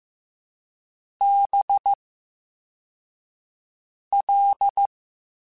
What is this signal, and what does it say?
Morse code, heard as a keyed tone, about 15 wpm, spaced as for 4 wpm, 800 Hz. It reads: BL
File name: Special INGs-mp3s / Special INGs-mp3s / VerySlowToFaster Ing @13